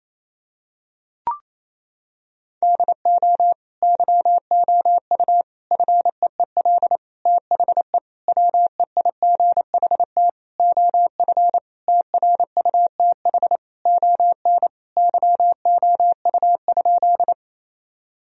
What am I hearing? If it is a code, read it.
Morse code, heard as a keyed tone, about 28 wpm, 700 Hz. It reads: DO YOU FEEL THE WEIGHT OF TRUTH ON YOU?
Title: Sentences from Top 1000 Words - Review Part 3 - 28wpm